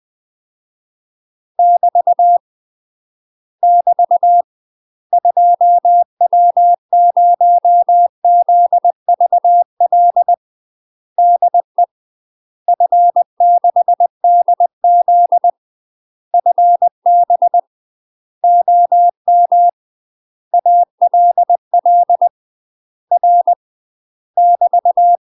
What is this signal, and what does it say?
Morse code, heard as a keyed tone, about 20 wpm, 700 Hz. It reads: <BT> <BT> 2W0ZVL DE F6DZ FB OM ALL R <BT>
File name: UK MASTER QSOs TRAINING / 20WPM / UK MASTER QSOs 0063